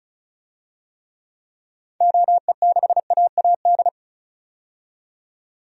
Morse code, heard as a keyed tone, about 34 wpm, 700 Hz. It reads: OE6AAD
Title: DXCC-100 Call Signs - 6 Characters - 35wpm